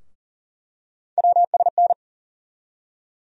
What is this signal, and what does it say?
Morse code, heard as a keyed tone, about 40 wpm, 700 Hz. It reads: WSN
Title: USA Call Sign Postfixes - 3 Letters 40wpm